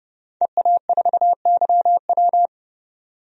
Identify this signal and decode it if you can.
Morse code, heard as a keyed tone, about 32 wpm, 700 Hz. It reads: EA4YW
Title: DXCC-100 Call Signs - 5 Characters - 30wpm